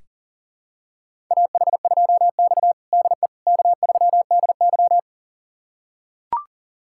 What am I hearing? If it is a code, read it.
Morse code, heard as a keyed tone, about 40 wpm, 700 Hz. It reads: AH2X DE K3DY
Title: USA Callsign DE Callsign - 4 Characters - 40wpm